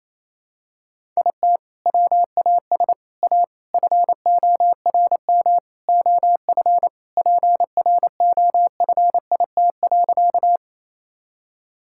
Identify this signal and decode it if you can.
Morse code, heard as a keyed tone, about 28 wpm, 700 Hz. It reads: IT WAS A FORM OF PROFIT.
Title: Sentences from Top 1000 Words - Review Part 5 - 28wpm